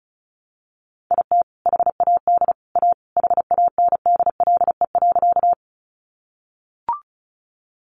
Morse code, heard as a keyed tone, about 34 wpm, 700 Hz. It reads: IT HAD A HANDLE.
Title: Sentences from Top 1100 Words 35wpm